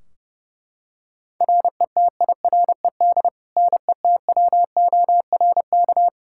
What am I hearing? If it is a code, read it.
Morse code, heard as a keyed tone, about 30 wpm, 700 Hz. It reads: RETIRED NETWORK
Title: K1USN SST QSO Practice 30wpm